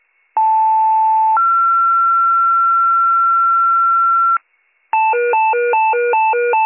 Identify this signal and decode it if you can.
Two-tone page, heard as a one-second tone, then a three-second tone, about 880 then 1360 Hz. It is heard from 0.4 to 4.4 s.